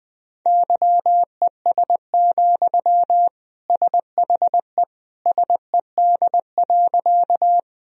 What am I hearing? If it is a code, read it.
Morse code, heard as a keyed tone, about 20 wpm, 700 Hz. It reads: YES, SHE SED.
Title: Sentences from Top 500 Words and Abbreviations 20wpm